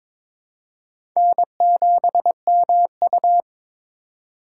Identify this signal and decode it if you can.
Morse code, heard as a keyed tone, about 22 wpm, 700 Hz. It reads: N7MU
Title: USA Call Signs - 2x Character Spacing - 22wpm